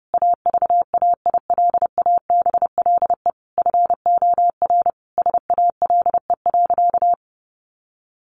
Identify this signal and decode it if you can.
Morse code, heard as a keyed tone, about 30 wpm, 700 Hz. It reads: AVAILABLE FOR SALE.
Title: Sentences from Top 1400 Words 30wpm